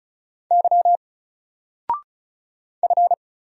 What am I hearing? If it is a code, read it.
Morse code, heard as a keyed tone, about 36 wpm, 700 Hz. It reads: Y F
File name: Single-Letter-Number-35wpm